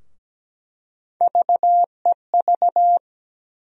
Morse code, heard as a keyed tone, about 17 wpm, 700 Hz. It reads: VEV
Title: USA Call Sign Postfixes - 3 Letters 17wpm